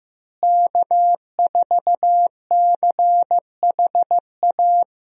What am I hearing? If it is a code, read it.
Morse code, heard as a keyed tone, about 15 wpm, 700 Hz. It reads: K4CHA
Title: USA Call Signs - 5 Characters - Repeated 2x - 15wpm